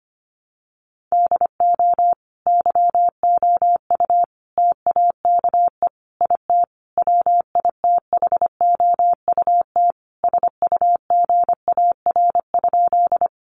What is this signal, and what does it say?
Morse code, heard as a keyed tone, about 25 wpm, 700 Hz. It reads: DO YOU TAKE IT WITHOUT SUGAR?
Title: Sentences from Top 1200 Words 25wpm